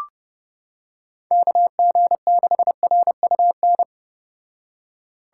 Morse code, heard as a keyed tone, about 30 wpm, 700 Hz. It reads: KG6RUN